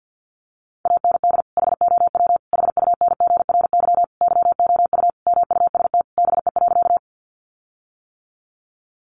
Morse code, heard as a keyed tone, about 50 wpm, 700 Hz. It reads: AND HOW HUNGRY YOU MUST BE.
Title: Sentences from Top 1400 Words 50wpm